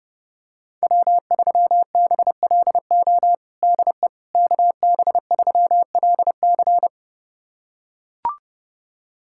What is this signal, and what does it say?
Morse code, heard as a keyed tone, about 30 wpm, 700 Hz. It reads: W3BLO DE KB3LC
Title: USA Callsign DE Callsign - 5 Characters - 30wpm